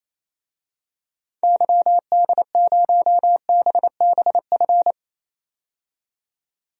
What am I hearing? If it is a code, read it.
Morse code, heard as a keyed tone, about 27 wpm, 700 Hz. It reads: YD0BBF
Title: DXCC-100 Call Signs - 6 Characters - 28wpm